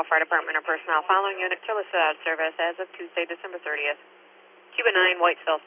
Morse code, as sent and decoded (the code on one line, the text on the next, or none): none